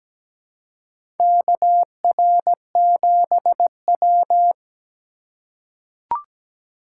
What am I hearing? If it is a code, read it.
Morse code, heard as a keyed tone, about 17 wpm, 700 Hz. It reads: KR7W
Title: K1USN SST QSO Practice - 17wpm